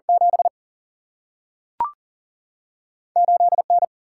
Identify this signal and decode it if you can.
Morse code, heard as a keyed tone, about 40 wpm, 700 Hz. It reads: 7 8 N
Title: Single-Letter Single-Number 40wpm